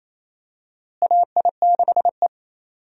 Morse code, heard as a keyed tone, about 28 wpm, 700 Hz. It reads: AI6E
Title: USA Call Signs - 4 Characters - Repeated 3x - 28wpm